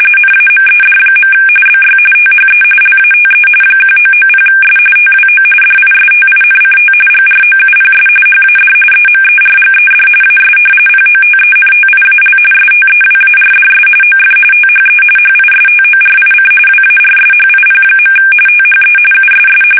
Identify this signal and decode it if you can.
RTTY, about 45.45 baud. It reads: NAMED. BY THE LATE 1990S TWO DISTINCT GROUPS HAD EMERGED AS POTENTIAL STEWARDS OF THE HISTORIC WARSHIP. THE USS
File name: fsk850 / battleshipnj